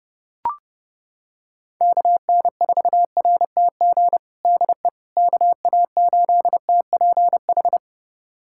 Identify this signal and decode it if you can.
Morse code, heard as a keyed tone, about 30 wpm, 700 Hz. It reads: KN4RTG DE KA8TPH
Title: USA Callsign DE Callsign - 6 Characters - 30wpm